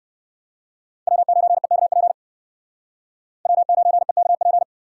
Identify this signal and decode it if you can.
Morse code, heard as a keyed tone, about 57 wpm, 700 Hz. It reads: W9EKP W9EKP